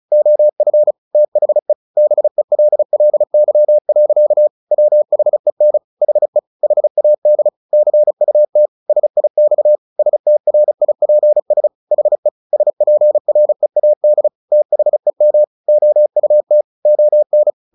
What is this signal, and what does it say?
Morse code, heard as a keyed tone, about 34 wpm, 600 Hz. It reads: OF THE BELLY. WHEN HE HAD CUT SIX STRIPS HE SPREAD THEM OUT ON
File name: the old man and the sea / the old man and the sea 1 35wpm